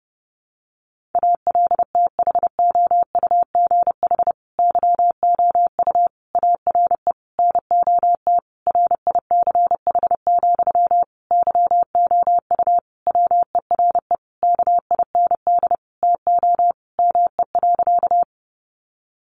Morse code, heard as a keyed tone, about 30 wpm, 700 Hz. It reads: ALTHOUGH YOU ARE NOT RICH, YOU WERE KIND TO ME.
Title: Sentences from Top 1100 Words 30wpm